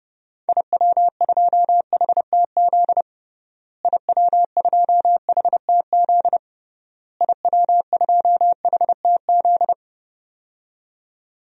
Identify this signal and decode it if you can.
Morse code, heard as a keyed tone, about 30 wpm, 700 Hz. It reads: IW2HTZ IW2HTZ IW2HTZ
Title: DXCC-100 Call Signs - 6 Characters - Repeated 3x - 30wpm